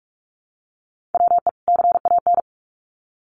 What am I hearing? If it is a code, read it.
Morse code, heard as a keyed tone, about 44 wpm, 700 Hz. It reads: WE CAN
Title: Sets of 2 Words 45wpm